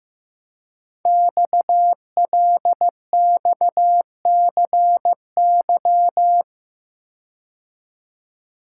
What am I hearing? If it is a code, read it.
Morse code, heard as a keyed tone, about 15 wpm, 700 Hz. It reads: XLXCY
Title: Code Groups - 5 Letters - 15wpm